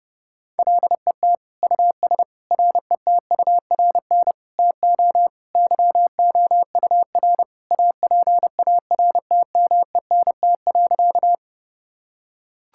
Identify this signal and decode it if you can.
Morse code, heard as a keyed tone, about 30 wpm, 700 Hz. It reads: LET US RETURN TO YOUR APARTMENT.
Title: Sentences from Top 1000 Words - Review Part 6 - 30wpm